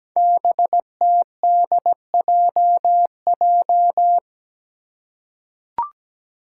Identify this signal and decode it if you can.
Morse code, heard as a keyed tone, about 17 wpm, 700 Hz. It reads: BTDJJ
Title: Five Challenging Letter-Number Combinations 17wpm